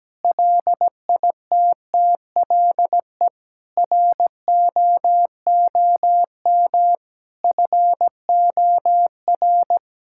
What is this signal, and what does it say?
Morse code, heard as a keyed tone, about 17 wpm, 700 Hz. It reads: LITTLE ROOM FOR
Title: Sentences from Top 1000 Words - Review Part 6 - 17wpm